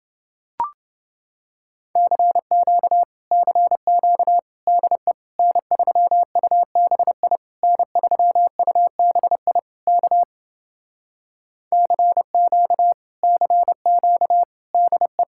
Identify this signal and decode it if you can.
Morse code, heard as a keyed tone, about 30 wpm, 700 Hz. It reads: CQ CQ DE N3UBI N3UBI K CQ CQ DE